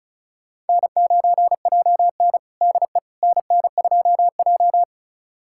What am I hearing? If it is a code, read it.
Morse code, heard as a keyed tone, about 36 wpm, 700 Hz. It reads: N9JN DE NN2J
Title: USA Callsign DE Callsign - 4 Characters - 35wpm